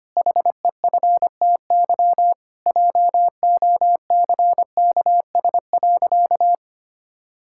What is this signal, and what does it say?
Morse code, heard as a keyed tone, about 25 wpm, 700 Hz. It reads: HEFTY JOCKS.
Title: Pangram Challenge v2 - 25wpm